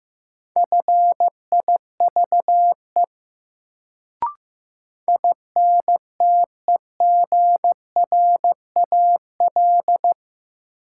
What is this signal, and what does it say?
Morse code, heard as a keyed tone, about 15 wpm, 700 Hz. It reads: FIVE INTEGRAL
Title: Sets of 5 Words - New Word Drill - 15wpm